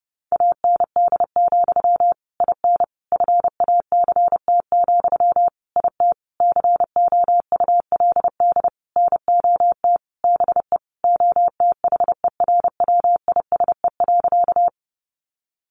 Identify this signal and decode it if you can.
Morse code, heard as a keyed tone, about 28 wpm, 700 Hz. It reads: AND, IN FACT, IT COULD NOT BE OTHERWISE.
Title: Sentences from Top 1200 Words 30wpm